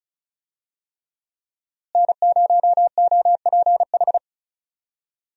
Morse code, heard as a keyed tone, about 36 wpm, 700 Hz. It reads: N0OPH
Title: USA Call Signs - 5 Characters - Repeated 3x - 35wpm